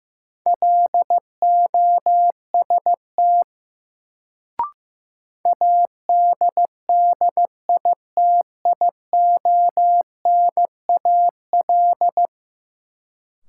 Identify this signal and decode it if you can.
Morse code, heard as a keyed tone, about 16 wpm, 700 Hz. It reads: LOST ADDITIONAL
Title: Sentences from Top 1400 Words - New Word Drill Alt - 15wpm